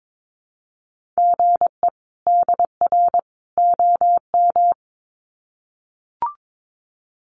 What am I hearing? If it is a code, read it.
Morse code, heard as a keyed tone, about 22 wpm, 700 Hz. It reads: GE DR OM
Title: Common QSO Phrases 22wpm